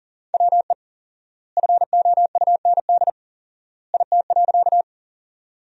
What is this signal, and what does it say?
Morse code, heard as a keyed tone, about 40 wpm, 700 Hz. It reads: WE FOUND IT.
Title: 4x Word-Spacing - Sentences from Top 300 Words - 40wpm